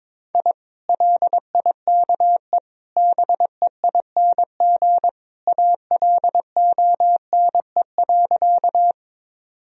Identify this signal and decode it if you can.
Morse code, heard as a keyed tone, about 22 wpm, 700 Hz. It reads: I LIKE BEING ALONE.
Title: Sentences from Top 1000 Words - Review Part 3 - 22wpm